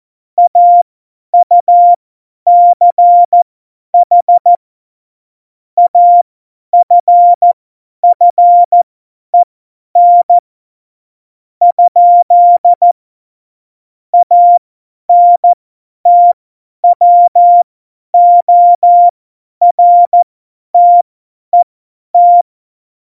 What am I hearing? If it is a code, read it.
Morse code, heard as a keyed tone, about 14 wpm, spaced as for 10 wpm, 700 Hz. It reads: AUCH AFFEN? ANTWORTET
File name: Morsekurs / Witze / Witz 30. Oktober 2018, 50 F70